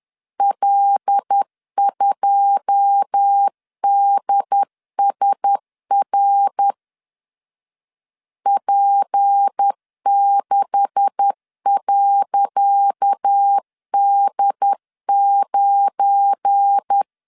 Morse code, heard as a keyed tone, about 10 wpm, 800 Hz. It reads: L2DSR P6.D9
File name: LEZIONI / 10 SIGNS / 10C DOT 10wpm 2940